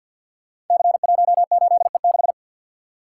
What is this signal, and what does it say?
Morse code, heard as a keyed tone, about 52 wpm, 700 Hz. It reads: C18E6
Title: Code Groups - 5 Letter:Numbers - 50wpm